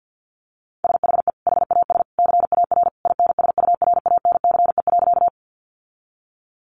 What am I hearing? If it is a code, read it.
Morse code, heard as a keyed tone, about 50 wpm, 700 Hz. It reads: SHE HAS CAR INSURANCE.
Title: Sentences from Top 1100 Words 50wpm